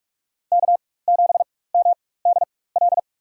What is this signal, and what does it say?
Morse code, heard as a keyed tone, about 46 wpm, spaced as for 21 wpm, 700 Hz. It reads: K7MDL
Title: USA Call Signs - 5 Characters - 4x Character Spacing - 45wpm